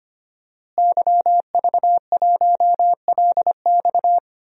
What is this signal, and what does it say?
Morse code, heard as a keyed tone, about 25 wpm, 700 Hz. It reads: YV1LX
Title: DXCC-100 Call Signs - 5 Characters - Repeated 2x - 25wpm